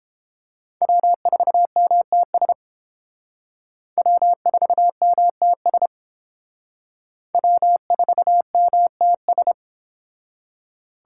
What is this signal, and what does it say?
Morse code, heard as a keyed tone, about 29 wpm, 700 Hz. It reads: W4MTS W4MTS W4MTS